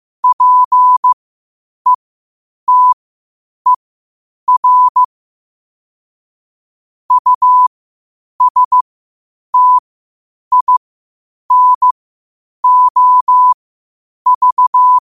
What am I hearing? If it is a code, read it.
Morse code, heard as a keyed tone, about 15 wpm, spaced as for 8 wpm, 1000 Hz. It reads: PETER USTINOV
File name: qotdfiles / 8wpm